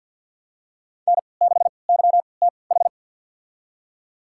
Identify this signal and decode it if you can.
Morse code, heard as a keyed tone, about 47 wpm, spaced as for 28 wpm, 700 Hz. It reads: N6YTH
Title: USA Call Signs - 5 Characters - 3x Character Spacing - 50wpm